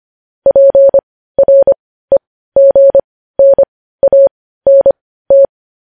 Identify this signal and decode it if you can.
Morse code, heard as a keyed tone, about 25 wpm, spaced as for 15 wpm, 550 Hz. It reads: PREGNANT
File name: podcast / News headlines-1x-25x15